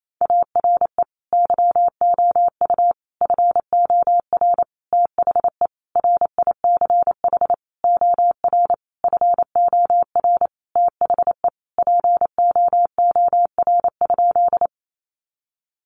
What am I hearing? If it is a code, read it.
Morse code, heard as a keyed tone, about 28 wpm, 700 Hz. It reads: ARE YOU FOR THE RICH OR FOR THE POOR?
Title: Sentences from Top 1100 Words 28wpm